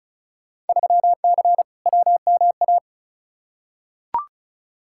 Morse code, heard as a keyed tone, about 36 wpm, 700 Hz. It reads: 3C WMA
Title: Field Day Exchanges Encore - 35wpm